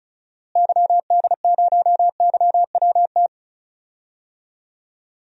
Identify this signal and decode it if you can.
Morse code, heard as a keyed tone, about 34 wpm, 700 Hz. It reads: YD0YWT
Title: DXCC-100 Call Signs - 6 Characters - Repeated 3x - 35wpm